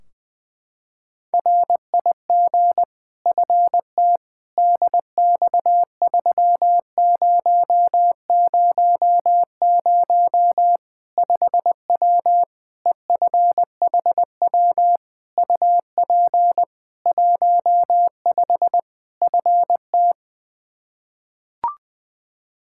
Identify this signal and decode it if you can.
Morse code, heard as a keyed tone, about 20 wpm, 700 Hz. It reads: RIG FT DX3000 5W EFHW UP 15 FT
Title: K1USN SST QSO Practice 20wpm